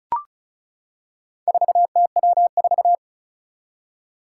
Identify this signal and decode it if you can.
Morse code, heard as a keyed tone, about 36 wpm, 700 Hz. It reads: 4TW4
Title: Four Challenging Letter-Number Combinations 35wpm